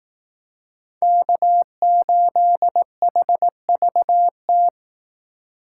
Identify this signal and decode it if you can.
Morse code, heard as a keyed tone, about 18 wpm, 700 Hz. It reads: K8HVT
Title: Speed Racing - USA 1x3 Call Signs 15wpm